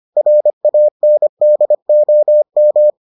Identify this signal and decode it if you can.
Morse code, heard as a keyed tone, about 25 wpm, 600 Hz. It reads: RANDOM